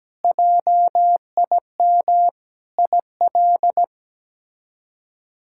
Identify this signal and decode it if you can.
Morse code, heard as a keyed tone, about 17 wpm, 700 Hz. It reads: JIM IL
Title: K1USN SST QSO Practice - 17wpm